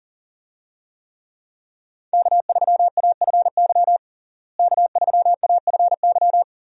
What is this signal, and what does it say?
Morse code, heard as a keyed tone, about 40 wpm, 700 Hz. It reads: K3AFY K3AFY
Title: USA Call Signs - 5 Characters - Repeated 2x - 40wpm